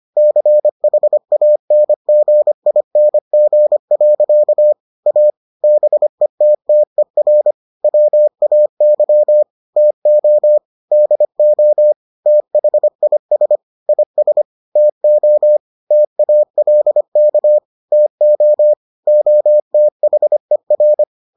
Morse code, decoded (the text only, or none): CHANGING. A BETTER WAY TO DO THIS IS TO TALK TO OTHER